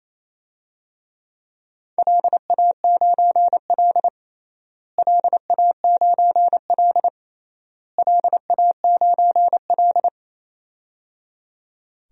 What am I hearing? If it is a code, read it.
Morse code, heard as a keyed tone, about 28 wpm, 700 Hz. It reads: LA9L LA9L LA9L
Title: DXCC-100 Call Signs - 4 Characters - Repeated 3x - 28wpm